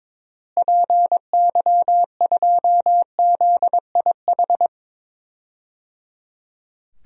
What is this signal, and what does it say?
Morse code, heard as a keyed tone, about 21 wpm, 700 Hz. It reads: PY2ZIH
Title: DXCC-100 Call Signs - 6 Characters - Repeated 3x - 22wpm